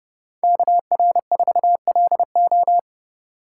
Morse code, heard as a keyed tone, about 30 wpm, 700 Hz. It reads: KR4LO